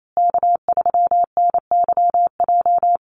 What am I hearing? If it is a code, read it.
Morse code, heard as a keyed tone, about 28 wpm, 700 Hz. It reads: K3NYJ
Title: USA Call Signs Repeated 3x 28wpm